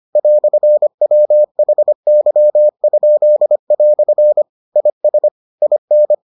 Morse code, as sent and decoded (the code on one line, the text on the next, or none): .-..-. .-- .... -.-- ..--.. .-..-. / .. ... / .. -.
"WHY?" IS IN